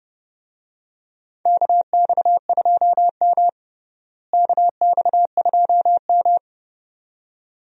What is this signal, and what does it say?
Morse code, heard as a keyed tone, about 32 wpm, 700 Hz. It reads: KX2M KX2M
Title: USA Call Signs - 4 Characters - Repeated 2x - 30wpm